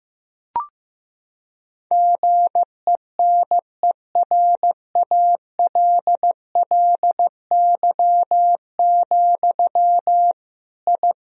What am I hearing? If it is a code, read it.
Morse code, heard as a keyed tone, about 15 wpm, 700 Hz. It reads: GENERALLY, I
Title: Sentences from Top 1000 Words - Review Part 4 - 15wpm